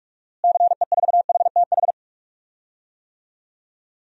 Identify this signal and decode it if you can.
Morse code, heard as a keyed tone, about 44 wpm, 700 Hz. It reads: CE4HTH
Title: DXCC-100 Call Signs - 6 Characters - Repeated 2x - 45wpm